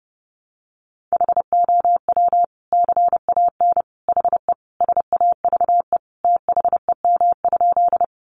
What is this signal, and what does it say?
Morse code, heard as a keyed tone, about 30 wpm, 700 Hz. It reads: HOW CAN HE SAVE THEM?
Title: Sentences from Top 1100 Words 30wpm